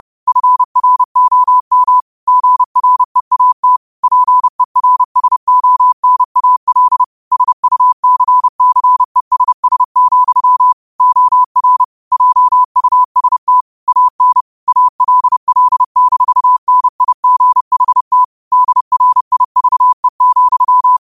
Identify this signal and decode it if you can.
Morse code, heard as a keyed tone, about 30 wpm, 1000 Hz. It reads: FROM GREAT PERSONAL SUCCESS, OR JUST AN ALL-NIGHT DRIVE,
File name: qotdfiles / 30wpm